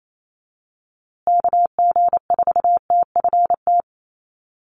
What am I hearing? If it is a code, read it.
Morse code, heard as a keyed tone, about 28 wpm, 700 Hz. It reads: KG4TFT
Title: USA Call Signs Repeated 2x 28wpm